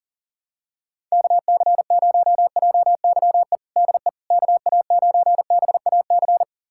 Morse code, heard as a keyed tone, about 40 wpm, 700 Hz. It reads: KC0JYE DE KA9BAC
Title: USA Callsign DE Callsign - 6 Characters - 40wpm